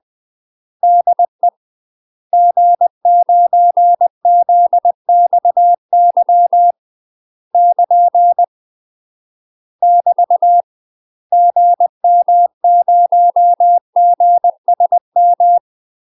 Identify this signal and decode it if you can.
Morse code, heard as a keyed tone, about 20 wpm, 700 Hz. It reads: DE G9ZXY <KN> <BT> GM0GSM
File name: UK MASTER QSOs TRAINING / 20WPM / UK MASTER QSOs 0059